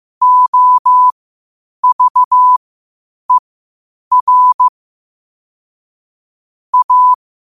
Morse code, heard as a keyed tone, about 15 wpm, spaced as for 8 wpm, 1000 Hz. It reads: OVER A
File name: qotdfiles / 8wpm